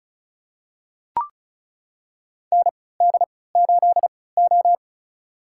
Morse code, heard as a keyed tone, about 36 wpm, spaced as for 20 wpm, 700 Hz. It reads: ND8O
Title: USA Call Signs - 4 Characters - 3x Character Spacing - 35wpm